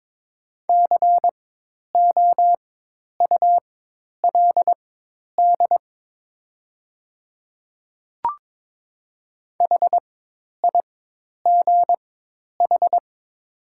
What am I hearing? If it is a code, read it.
Morse code, heard as a keyed tone, about 22 wpm, spaced as for 10 wpm, 700 Hz. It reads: COULD HIGH
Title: 4x Character Spacing - Top 300 Words - 22wpm